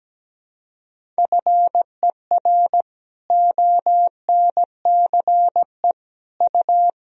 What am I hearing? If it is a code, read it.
Morse code, heard as a keyed tone, about 17 wpm, 700 Hz. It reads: FER ONCE U
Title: Sentences from Top 300 Words and Abbreviations 17wpm